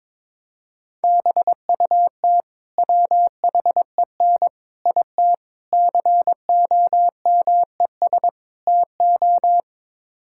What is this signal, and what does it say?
Morse code, heard as a keyed tone, about 22 wpm, 700 Hz. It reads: BUT WHEN IT COMES TO
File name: practice / 22wpm / Build A Phrase - 5-Word Phrases - 22wpm